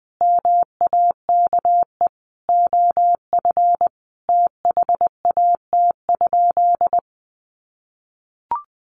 Morse code, heard as a keyed tone, about 20 wpm, 700 Hz. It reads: MAKE OF THAT?
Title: Sentences from Top 100 Words 20wpm